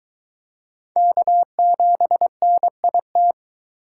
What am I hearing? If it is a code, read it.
Morse code, heard as a keyed tone, about 22 wpm, 700 Hz. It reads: K7NIT